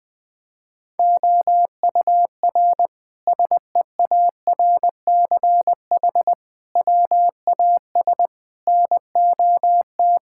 Morse code, heard as a keyed tone, about 21 wpm, 700 Hz. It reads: OUR SEARCH WAS NOT
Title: Sentences from Top 1000 Words - Review Part 5 - 20wpm